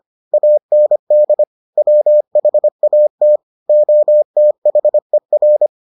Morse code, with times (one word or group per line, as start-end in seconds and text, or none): 0.3-1.4 AND
1.8-3.4 WHAT
3.7-5.7 OTHER